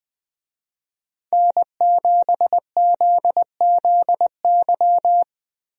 Morse code, heard as a keyed tone, about 21 wpm, 700 Hz. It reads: N7ZZY
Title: USA Call Signs - 5 Characters - Repeated 3x - 20wpm